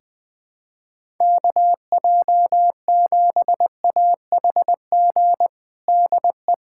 Morse code, heard as a keyed tone, about 21 wpm, 700 Hz. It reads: KJ7AHG DE
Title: USA Callsign DE Callsign 20wpm